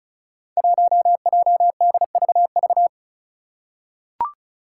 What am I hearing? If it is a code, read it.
Morse code, heard as a keyed tone, about 36 wpm, 700 Hz. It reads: 1JDVV
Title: Five Challenging Letter-Number Combinations 35wpm